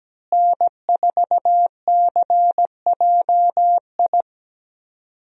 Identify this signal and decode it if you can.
Morse code, heard as a keyed tone, about 17 wpm, 700 Hz. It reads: N4CJI
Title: USA Call Signs - 5 Characters - Repeated 2x - 17wpm